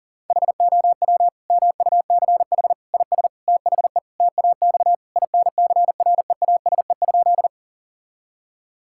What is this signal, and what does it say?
Morse code, heard as a keyed tone, about 40 wpm, 700 Hz. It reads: HOW MUCH IS THE TAX INCREASE?
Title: Sentences from Top 1000 Words - Review Part 5 - 40wpm